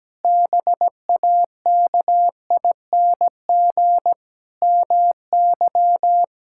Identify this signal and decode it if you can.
Morse code, heard as a keyed tone, about 17 wpm, 700 Hz. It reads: BAKING MY
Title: Pangram Challenge v2 - 17wpm